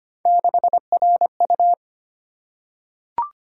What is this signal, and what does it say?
Morse code, heard as a keyed tone, about 25 wpm, 700 Hz. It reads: 6RU
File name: Three Letter-Number Combinations - 25wpm